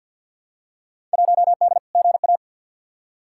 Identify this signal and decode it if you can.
Morse code, heard as a keyed tone, about 52 wpm, 700 Hz. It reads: 1D GA
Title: Field Day Exchanges - 50wpm